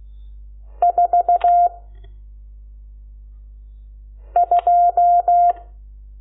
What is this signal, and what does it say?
Morse code, heard as a keyed tone, about 16 wpm, 700 Hz. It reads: 4 2
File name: Lecciones / CLASE Nº12